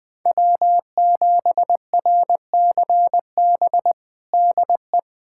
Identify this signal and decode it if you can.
Morse code, heard as a keyed tone, about 20 wpm, 700 Hz. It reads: W7RCB DE
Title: USA Callsign DE Callsign - 5 Characters - 20wpm